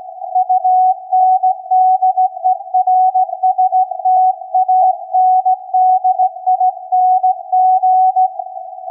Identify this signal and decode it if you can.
Morse code, heard as a keyed tone, about 16 wpm, 740 Hz. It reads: UNDERSTANDING
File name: iz7slz / VLF / SAQ 16-11-2022